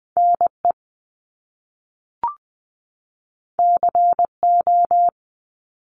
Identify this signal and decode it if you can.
Morse code, heard as a keyed tone, about 20 wpm, 700 Hz. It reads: NE CO
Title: Top Two Letter Combinations in Words 20wpm